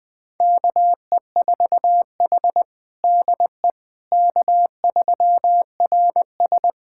Morse code, heard as a keyed tone, about 20 wpm, 700 Hz. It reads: KE4H DE K3RS
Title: USA Callsign DE Callsign - 4 Characters - 20wpm